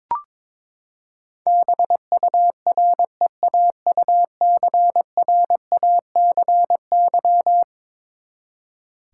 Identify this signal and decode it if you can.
Morse code, heard as a keyed tone, about 22 wpm, 700 Hz. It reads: BUREAUCRACY
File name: practice / 22wpm / Eleven Letter Words - 22wpm